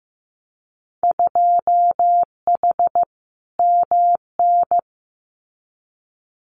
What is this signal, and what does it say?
Morse code, heard as a keyed tone, about 15 wpm, 700 Hz. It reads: 2H MN
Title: Winter Field Day Exchanges 15wpm